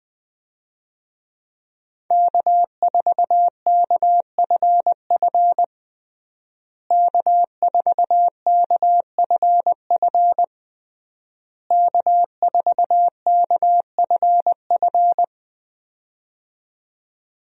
Morse code, heard as a keyed tone, about 20 wpm, 700 Hz. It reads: K4KFF K4KFF K4KFF
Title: USA Call Signs - 5 Characters - Repeated 3x - 20wpm